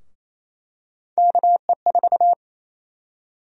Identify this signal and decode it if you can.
Morse code, heard as a keyed tone, about 28 wpm, 700 Hz. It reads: KE4
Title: USA Call Sign Prefixes - 28wpm